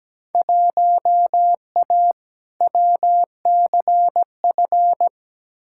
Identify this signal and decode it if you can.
Morse code, heard as a keyed tone, about 17 wpm, 700 Hz. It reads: 1A WCF
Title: Field Day Exchanges - 17wpm